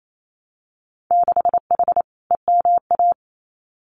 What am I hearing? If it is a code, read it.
Morse code, heard as a keyed tone, about 28 wpm, 700 Hz. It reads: 6H EMA
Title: Winter Field Day Exchanges Encore 28wpm